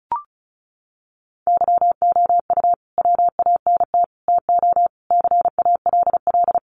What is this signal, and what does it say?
Morse code, heard as a keyed tone, about 34 wpm, 700 Hz. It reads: YOU WANT TO CALL
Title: Sets of 5 Words Encore2 35wpm